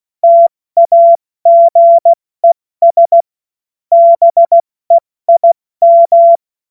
Morse code, heard as a keyed tone, about 16 wpm, spaced as for 14 wpm, 680 Hz. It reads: TAGES BEIM
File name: Morsekurs / Witze / Witz 05 Dezember 2023 70 F80